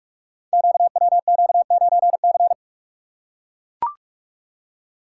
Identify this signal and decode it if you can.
Morse code, heard as a keyed tone, about 44 wpm, 700 Hz. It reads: QWQ9C
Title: Code Groups - 5 Letter:Numbers - 45wpm